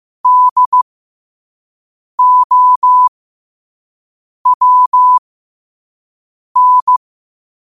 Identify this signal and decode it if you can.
Morse code, heard as a keyed tone, about 15 wpm, spaced as for 5 wpm, 1000 Hz. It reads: DOWN
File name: qotdfiles / 5wpm